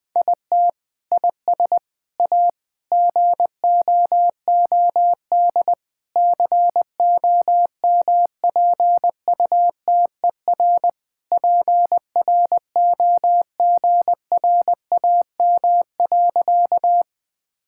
Morse code, heard as a keyed tone, about 21 wpm, 700 Hz. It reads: IT IS A GOOD COMPUTER PROGRAM.
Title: Sentences from Top 1000 Words - Review Part 5 - 20wpm